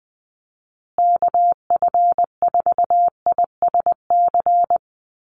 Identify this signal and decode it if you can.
Morse code, heard as a keyed tone, about 20 wpm, 700 Hz. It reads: KF4ISC
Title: USA Call Signs Repeated 3x 20wpm